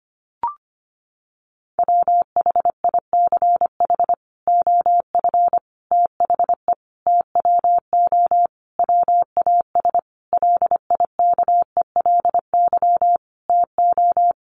Morse code, heard as a keyed tone, about 25 wpm, 700 Hz. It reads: WHICH OF THE TWO WAS LIKELY TO
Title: Sentences from Top 1300 Words 25wpm